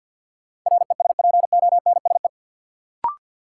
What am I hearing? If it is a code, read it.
Morse code, heard as a keyed tone, about 47 wpm, 700 Hz. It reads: RESPONSE
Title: Sentences from Top 1400 Words - New Word Drill Alt - 50wpm